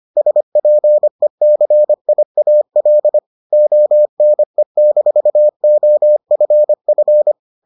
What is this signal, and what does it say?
Morse code, heard as a keyed tone, about 25 wpm, 600 Hz. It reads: SPECIAL ONE-OFF